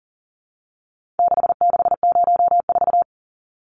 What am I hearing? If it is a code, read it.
Morse code, heard as a keyed tone, about 39 wpm, 700 Hz. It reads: 6604
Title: Numbers - 4-Digit 40wpm